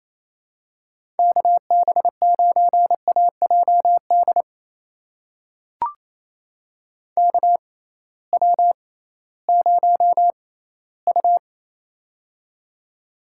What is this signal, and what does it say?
Morse code, heard as a keyed tone, about 28 wpm, spaced as for 16 wpm, 700 Hz. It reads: KB9AJD KW0U
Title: USA Call Signs - 6x Character Spacing - 28wpm